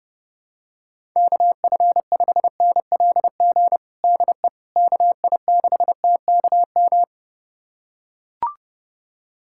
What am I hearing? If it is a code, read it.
Morse code, heard as a keyed tone, about 32 wpm, 700 Hz. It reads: KF5NLG DE KI6TKM
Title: USA Callsign DE Callsign - 6 Characters - 30wpm